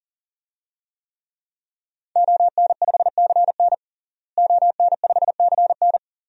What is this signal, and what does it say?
Morse code, heard as a keyed tone, about 40 wpm, 700 Hz. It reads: ON5CN ON5CN
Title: DXCC-100 Call Signs - 5 Characters - Repeated 3x - 40wpm